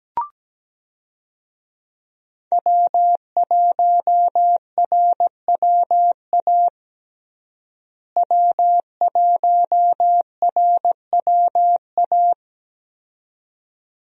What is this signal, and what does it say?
Morse code, heard as a keyed tone, about 17 wpm, 700 Hz. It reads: W1RWA W1RWA
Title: USA Call Signs - 5 Characters - Repeated 2x - 17wpm